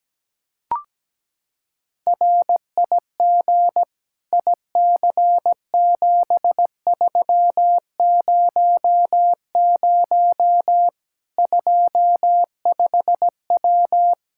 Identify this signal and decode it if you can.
Morse code, heard as a keyed tone, about 17 wpm, 700 Hz. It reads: RIG IC7300 25W
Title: Practice QSO - Exchange 2 - Info - 17wpm